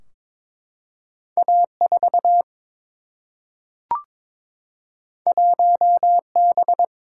Morse code, heard as a keyed tone, about 22 wpm, 700 Hz. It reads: A4 1B
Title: Single-Letter Single-Number 22wpm